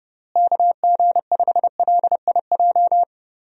Morse code, heard as a keyed tone, about 32 wpm, 700 Hz. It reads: KG5LIJ
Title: USA Call Signs - 6 Characters - Repeated 3x - 30wpm